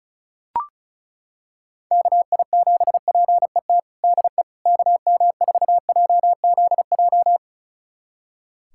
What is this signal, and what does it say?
Morse code, heard as a keyed tone, about 36 wpm, 700 Hz. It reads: KI7PET DE KM4JZJ
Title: USA Callsign DE Callsign - 6 Characters - 35wpm